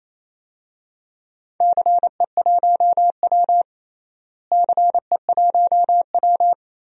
Morse code, heard as a keyed tone, about 28 wpm, 700 Hz. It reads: CE1W CE1W
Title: DXCC-100 Call Signs - 4 Characters - Repeated 2x - 28wpm